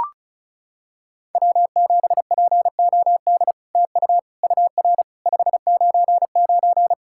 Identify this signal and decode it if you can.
Morse code, heard as a keyed tone, about 34 wpm, 700 Hz. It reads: W7POD TU UR 599
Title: SOTA QSO Practice 35wpm